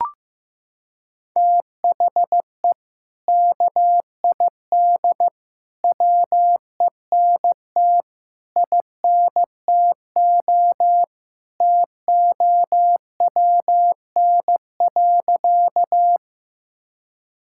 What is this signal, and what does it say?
Morse code, heard as a keyed tone, about 15 wpm, 700 Hz. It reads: THE KID WENT INTO TOWN.
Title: Sentences from Top 1600 Words 15wpm